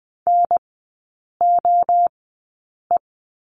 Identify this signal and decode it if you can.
Morse code, heard as a keyed tone, about 20 wpm, spaced as for 8 wpm, 690 Hz. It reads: NOE
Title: 081 Lesson 14e - Number 2 - Three Characters 20wpm